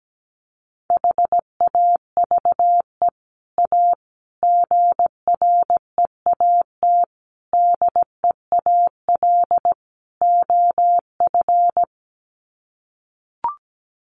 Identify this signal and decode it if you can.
Morse code, heard as a keyed tone, about 17 wpm, 700 Hz. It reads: HAVE A GREAT DEAL OF
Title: Sets of 5 Words Encore2 17wpm